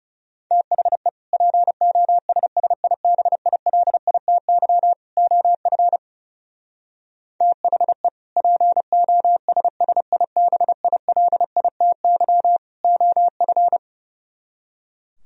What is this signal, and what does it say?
Morse code, heard as a keyed tone, about 37 wpm, 700 Hz. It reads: THE POSSIBILITY OF THE POSSIBILITY OF
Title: Speed-Racing - Sets of 3 Words 30wpm